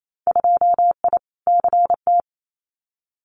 Morse code, heard as a keyed tone, about 28 wpm, 700 Hz. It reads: 2I CT
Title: Winter Field Day Exchanges 28wpm